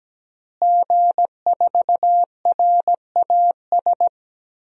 Morse code, heard as a keyed tone, about 17 wpm, 700 Hz. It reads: G4RAS
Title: DXCC-100 Call Signs - 5 Characters - Repeated 2x - 17wpm